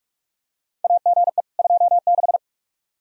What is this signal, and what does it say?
Morse code, heard as a keyed tone, about 47 wpm, 700 Hz. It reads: AGE 26
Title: Practice QSO - Exchange 3 - More Info - 45wpm